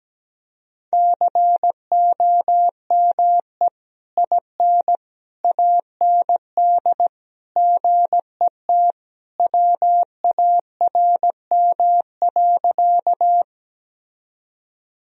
Morse code, heard as a keyed tone, about 17 wpm, 700 Hz. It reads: COME IN AND GET WARM.
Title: Sentences from Top 1000 Words - Review Part 2 - 17wpm